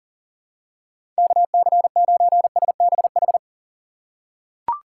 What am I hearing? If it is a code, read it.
Morse code, heard as a keyed tone, about 40 wpm, 700 Hz. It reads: KC9SBH
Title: USA Call Signs - 6 Characters - 2x Character Spacing - 40wpm